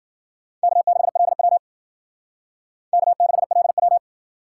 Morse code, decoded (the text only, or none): K6LW K6LW